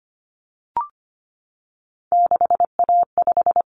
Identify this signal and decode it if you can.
Morse code, heard as a keyed tone, about 25 wpm, 700 Hz. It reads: 6A5
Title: Common Cut Numbers - 3-Digit 25wpm